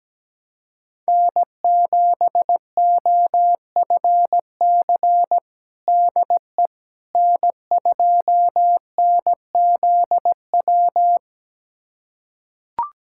Morse code, heard as a keyed tone, about 17 wpm, 700 Hz. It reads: N7OFC DE N2NZW
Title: USA Callsign DE Callsign - 5 Characters - 17wpm